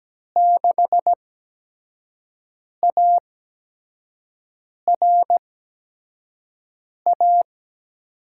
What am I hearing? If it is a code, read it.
Morse code, heard as a keyed tone, about 17 wpm, spaced as for 5 wpm, 700 Hz. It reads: 6ARA
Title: USA Call Signs - 8x Character Spacing - 17wpm